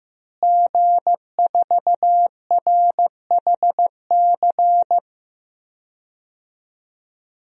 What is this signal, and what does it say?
Morse code, heard as a keyed tone, about 15 wpm, 700 Hz. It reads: G4RHC
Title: DXCC-100 Call Signs - 5 Characters - Repeated 3x - 15wpm